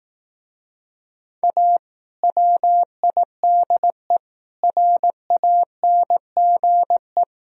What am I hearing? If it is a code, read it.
Morse code, heard as a keyed tone, about 18 wpm, 700 Hz. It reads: A WIDE RANGE
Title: Speed-Racing - Sets of 4 Words 15wpm